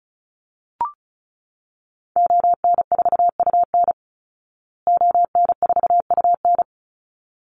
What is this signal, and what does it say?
Morse code, heard as a keyed tone, about 34 wpm, 700 Hz. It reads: ON4UN ON4UN
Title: Call Signs Repeated 3x - 35wpm